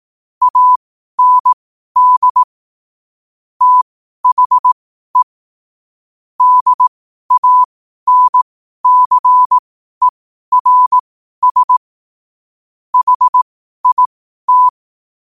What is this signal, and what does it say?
Morse code, heard as a keyed tone, about 18 wpm, spaced as for 12 wpm, 1000 Hz. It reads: AND THE DANCERS HIT
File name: qotdfiles / 12wpm